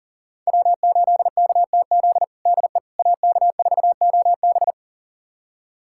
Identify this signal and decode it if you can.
Morse code, heard as a keyed tone, about 40 wpm, 700 Hz. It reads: W8KTZ DE AK4OB
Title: USA Callsign DE Callsign - 5 Characters - 40wpm